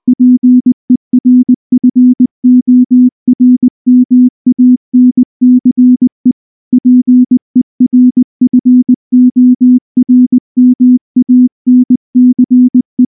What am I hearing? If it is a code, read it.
Morse code, heard as a keyed tone, about 20 wpm, 250 Hz. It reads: PERFORMANCE PERFORMANCE
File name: morse / morse-training-20WPM-6